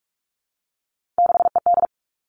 Morse code, heard as a keyed tone, about 44 wpm, 700 Hz. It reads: 6ED
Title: All Cut Numbers - 3-Digit 45wpm